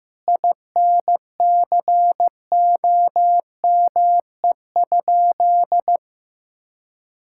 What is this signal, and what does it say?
Morse code, heard as a keyed tone, about 15 wpm, 700 Hz. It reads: INCOME?
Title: Sentences from Top 1000 Words - Review Part 5 - 15wpm